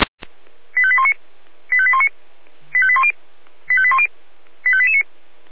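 Five-tone selective call (ZVEI), heard as quick runs of five tones, tones of about 70 ms. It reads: 85819 85819 85810 85810 85889
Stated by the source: ZVEI five-tone call received by SDR, recorded by Feuerwehr Orlamünde.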